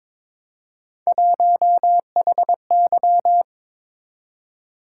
Morse code, heard as a keyed tone, about 26 wpm, 700 Hz. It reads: 1HY
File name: Speed-Racing - Three Letter-Number Combinations - 22wpm